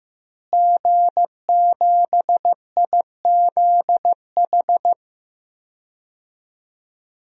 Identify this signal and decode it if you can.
Morse code, heard as a keyed tone, about 14 wpm, 700 Hz. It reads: G7IZH
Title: DXCC-100 Call Signs - 5 Characters - Repeated 2x - 15wpm